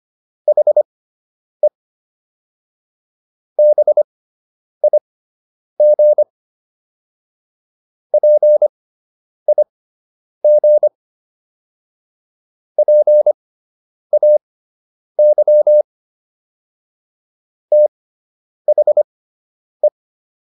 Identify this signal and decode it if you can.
Morse code, heard as a keyed tone, about 25 wpm, spaced as for 9 wpm, 600 Hz. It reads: HE BIG PIG PAY THE